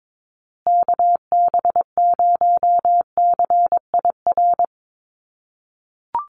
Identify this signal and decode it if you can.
Morse code, heard as a keyed tone, about 22 wpm, 700 Hz. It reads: KB0CIR
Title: USA Call Signs Repeated 3x 22wpm